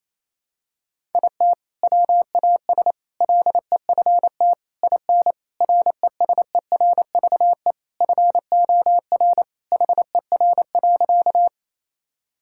Morse code, heard as a keyed tone, about 28 wpm, 700 Hz. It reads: IT WAS LEFT IN RESERVE FOR HER.
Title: Sentences from Top 1000 Words - Review Part 6 - 28wpm